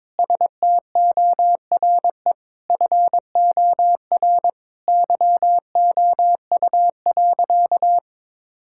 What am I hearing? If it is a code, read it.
Morse code, heard as a keyed tone, about 22 wpm, 700 Hz. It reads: STORE FOR YOU.
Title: Sentences from Top 1000 Words - Review Part 5 - 22wpm